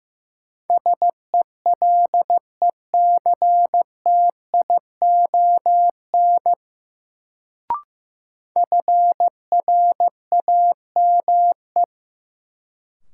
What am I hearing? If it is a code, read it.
Morse code, heard as a keyed tone, about 15 wpm, 700 Hz. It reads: SELECTION FRAME
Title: Sentences from Top 1400 Words - New Word Drill Alt - 15wpm